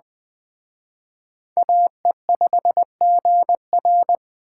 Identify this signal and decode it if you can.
Morse code, heard as a keyed tone, about 21 wpm, 700 Hz. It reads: AE5GR
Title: Speed Racing - USA 2x2 Call Signs 17wpm